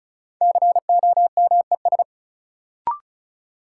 Morse code, heard as a keyed tone, about 36 wpm, 700 Hz. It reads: COMES
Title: Sentences from Top 1000 Words - Review Part 5 - New Word Drill - 35wpm